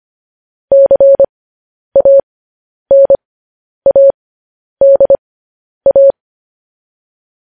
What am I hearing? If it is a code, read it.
Morse code, heard as a keyed tone, about 25 wpm, spaced as for 10 wpm, 550 Hz. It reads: CANADA
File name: podcast / News headlines-1x-25x10